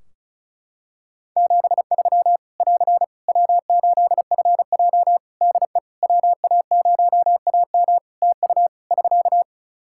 Morse code, heard as a keyed tone, about 36 wpm, 700 Hz. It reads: 73 <AR> W8FJ DE WA0AM TU <SK>
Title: Practice QSO - Exchange 4 - Ending - 35wpm